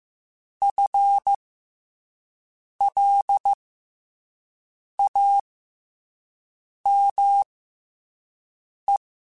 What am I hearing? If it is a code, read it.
Morse code, heard as a keyed tone, about 15 wpm, spaced as for 5 wpm, 800 Hz. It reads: FLAME
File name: FL Runs / FL Runs mp3s / FL Runs @5